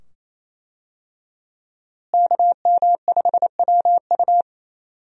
Morse code, heard as a keyed tone, about 27 wpm, 700 Hz. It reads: KM5WU
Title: USA Call Signs - 5 Characters - Repeated 2x - 28wpm